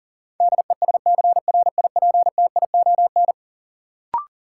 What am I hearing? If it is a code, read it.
Morse code, heard as a keyed tone, about 40 wpm, 700 Hz. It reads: DESCRIPTION
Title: Sentences from Top 1300 Words - New Word Drill - 40wpm